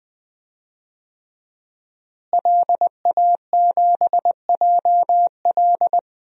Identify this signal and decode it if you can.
Morse code, heard as a keyed tone, about 20 wpm, 700 Hz. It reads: LA7JL
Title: DXCC-100 Call Signs - 5 Characters - Repeated 3x - 20wpm